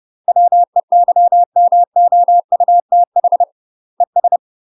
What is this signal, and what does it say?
Morse code, heard as a keyed tone, about 30 wpm, 700 Hz. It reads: WEYMOUTH ES
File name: UK MASTER QSOs TRAINING / 30WPM / UK MASTER QSOs 0033